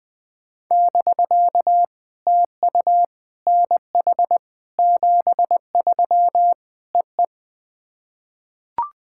Morse code, heard as a keyed tone, about 20 wpm, 700 Hz. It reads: <BK> TU NH 73 EE
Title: POTA QSO Practice - 20wpm